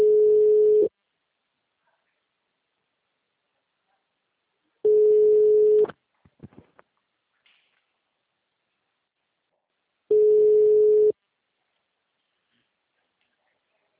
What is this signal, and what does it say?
Ringback tone (Europe, 425 Hz), with speech.